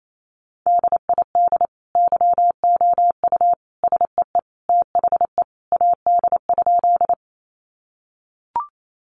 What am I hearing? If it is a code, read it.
Morse code, heard as a keyed tone, about 28 wpm, 700 Hz. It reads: DID YOU SEE THE AD?
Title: Sentences from Top 1300 Words 28wpm